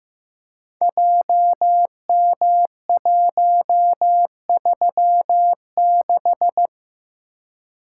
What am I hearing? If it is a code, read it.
Morse code, heard as a keyed tone, about 15 wpm, 700 Hz. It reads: JM136
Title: Code Groups - 5 Letter:Numbers - 15wpm